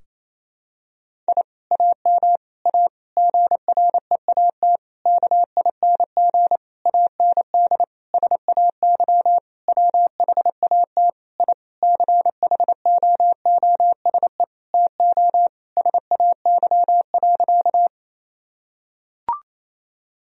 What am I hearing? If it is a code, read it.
Morse code, heard as a keyed tone, about 28 wpm, 700 Hz. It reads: I AM A GREAT KING AND SAY WHAT I CHOOSE TO SAY.
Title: Sentences from Top 1000 Words - Review Part 4 - 28wpm